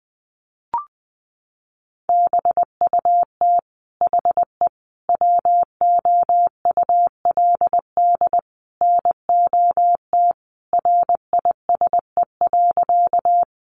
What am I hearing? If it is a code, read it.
Morse code, heard as a keyed tone, about 20 wpm, 700 Hz. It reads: BUT HE WOULD NOT RISE.
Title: Sentences from Top 900 Words 20wpm